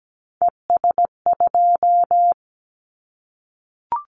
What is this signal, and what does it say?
Morse code, heard as a keyed tone, about 17 wpm, 700 Hz. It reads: ES2
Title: Callsign Prefixes for Top 100 DXCC - 17wpm